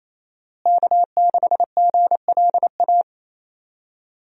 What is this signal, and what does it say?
Morse code, heard as a keyed tone, about 28 wpm, 700 Hz. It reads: K6GLA
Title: USA Call Signs - 5 Characters - Repeated 3x - 28wpm